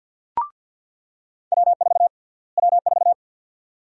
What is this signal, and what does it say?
Morse code, heard as a keyed tone, about 47 wpm, 700 Hz. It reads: W4 W4
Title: USA Call Sign Prefixes - Repeated 2x - 50wpm